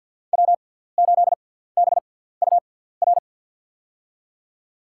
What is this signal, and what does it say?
Morse code, heard as a keyed tone, about 53 wpm, spaced as for 17 wpm, 700 Hz. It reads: W8BUR
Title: USA Call Signs - 6x Character Spacing - 50wpm